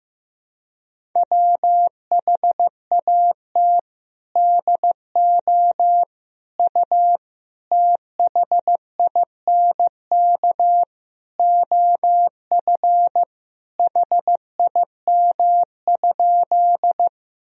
Morse code, heard as a keyed tone, about 15 wpm, 700 Hz. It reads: WHAT DO U THINK OF HIM?
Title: Sentences from Top 200 Words and Abbreviations 15wpm